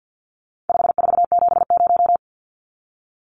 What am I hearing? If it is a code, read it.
Morse code, heard as a keyed tone, about 50 wpm, 700 Hz. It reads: 5470
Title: Numbers - 4-Digit 50wpm